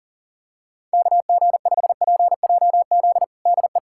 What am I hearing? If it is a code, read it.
Morse code, heard as a keyed tone, about 40 wpm, 700 Hz. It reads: KG5PJZ DE
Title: USA Callsign DE Callsign - 6 Characters - 40wpm